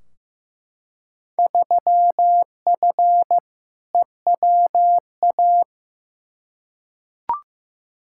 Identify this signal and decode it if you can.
Morse code, heard as a keyed tone, about 15 wpm, 700 Hz. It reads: 3F EWA
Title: Field Day Exchanges Encore - 15wpm